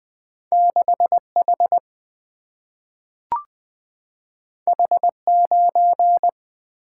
Morse code, heard as a keyed tone, about 20 wpm, 700 Hz. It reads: 6 H H 9
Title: Single-Letter Single-Number 20wpm